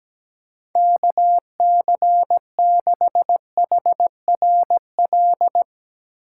Morse code, heard as a keyed tone, about 17 wpm, 700 Hz. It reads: KC6HRL
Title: USA Call Signs - 3x Character Spacing - 17wpm